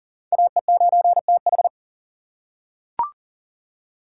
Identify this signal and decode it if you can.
Morse code, heard as a keyed tone, about 40 wpm, 700 Hz. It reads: AE9TH
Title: USA Call Signs - 5 Characters - 6x Character Spacing - 40wpm